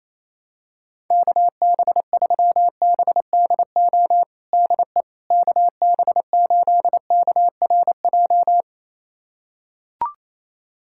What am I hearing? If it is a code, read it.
Morse code, heard as a keyed tone, about 28 wpm, 700 Hz. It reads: KB3BDO DE KB8KRJ
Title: USA Callsign DE Callsign 28wpm